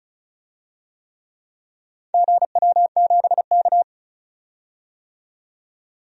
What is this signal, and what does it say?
Morse code, heard as a keyed tone, about 36 wpm, 700 Hz. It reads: GW7K
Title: DXCC-100 Call Signs - 4 Characters - 35wpm